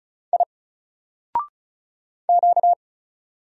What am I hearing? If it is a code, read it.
Morse code, heard as a keyed tone, about 35 wpm, 700 Hz. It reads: I Q
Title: Single Letter-Number 35wpm